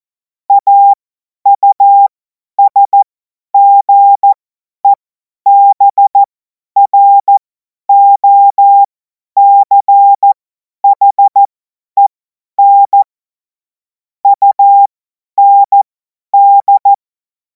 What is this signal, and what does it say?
Morse code, heard as a keyed tone, about 14 wpm, spaced as for 10 wpm, 800 Hz. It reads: AUSGEBROCHEN UND
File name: Morsekurs / Witze / Witz 05. Mai 2026 50 F70